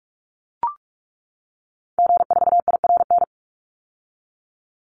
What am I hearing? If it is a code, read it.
Morse code, heard as a keyed tone, about 44 wpm, 700 Hz. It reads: G4IRN
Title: Call Signs - 45wpm